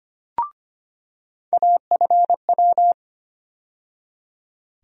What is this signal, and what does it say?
Morse code, heard as a keyed tone, about 25 wpm, 700 Hz. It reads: AFW
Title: USA Call Sign Postfixes - 3 Letters 25wpm